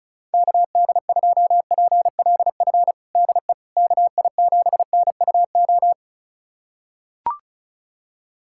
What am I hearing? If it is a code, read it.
Morse code, heard as a keyed tone, about 36 wpm, 700 Hz. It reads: KD2PLF DE KI7NUO
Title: USA Callsign DE Callsign - 6 Characters - 35wpm